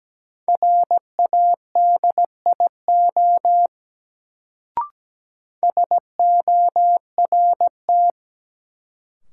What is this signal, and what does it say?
Morse code, heard as a keyed tone, about 17 wpm, 700 Hz. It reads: RADIO SORT
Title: Sentences from Top 1000 Words - Review Part 3 - New Word Drill - 17wpm